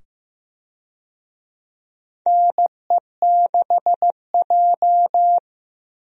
Morse code, heard as a keyed tone, about 15 wpm, 700 Hz. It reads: NE6J
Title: USA Call Signs - 4 Characters - 15wpm